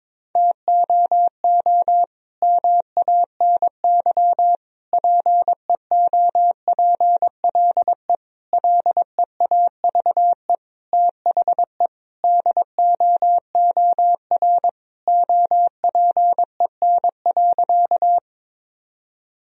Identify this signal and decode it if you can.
Morse code, heard as a keyed tone, about 22 wpm, 700 Hz. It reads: TOO MANY PEOPLE LEAVE THE DOOR OPEN.
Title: Sentences from Top 1000 Words - Review Part 1 - 22wpm